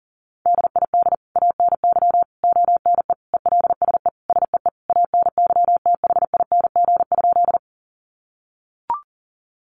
Morse code, heard as a keyed tone, about 40 wpm, 700 Hz. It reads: DID ANY ONE ELSE SEE ANYTHING?
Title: Sentences from Top 1000 Words 40wpm